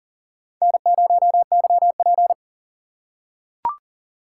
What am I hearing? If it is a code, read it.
Morse code, heard as a keyed tone, about 41 wpm, 700 Hz. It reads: N0YP
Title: USA Call Signs - 4 Characters - 3x Character Spacing - 40wpm